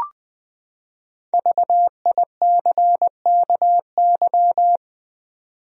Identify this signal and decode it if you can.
Morse code, heard as a keyed tone, about 20 wpm, 700 Hz. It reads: VICKY
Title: USA Common Names - Max 5 Letters - 20wpm